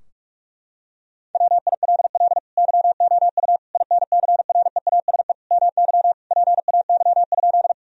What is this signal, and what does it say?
Morse code, heard as a keyed tone, about 46 wpm, 700 Hz. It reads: WILL YOU INCREASE MY PAY?
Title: Sentences from Top 1000 Words - Review Part 4 - 45wpm